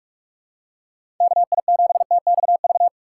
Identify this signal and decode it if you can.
Morse code, heard as a keyed tone, about 46 wpm, 700 Hz. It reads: KI7TXV
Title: USA Call Signs - 6 Characters - 2x Character Spacing - 45wpm